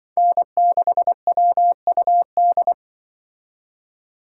Morse code, heard as a keyed tone, about 25 wpm, 700 Hz. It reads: N6WUD
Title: Speed Racing - USA 1x3 Call Signs 20wpm